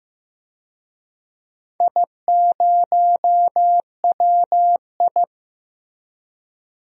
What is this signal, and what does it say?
Morse code, heard as a keyed tone, about 15 wpm, 700 Hz. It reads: I0WI
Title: DXCC-100 Call Signs - 4 Characters - 15wpm